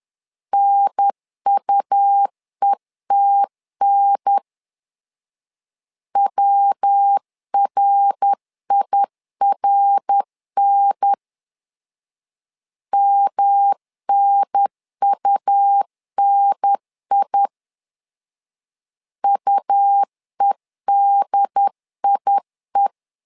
Morse code, heard as a keyed tone, about 10 wpm, 800 Hz. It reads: NUETN WRIRN MNUNI UEDIE
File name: LEZIONI / 3 MWRO / 3 MWR 10wpm 1952